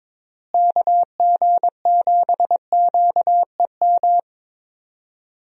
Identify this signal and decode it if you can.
Morse code, heard as a keyed tone, about 22 wpm, 700 Hz. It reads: KG7QEM
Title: USA Call Signs - 6 Characters - Repeated 3x - 22wpm